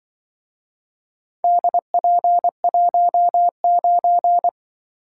Morse code, heard as a keyed tone, about 24 wpm, 700 Hz. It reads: DP19
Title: Speed-Racing - Four Letter-Number Combinations - 20wpm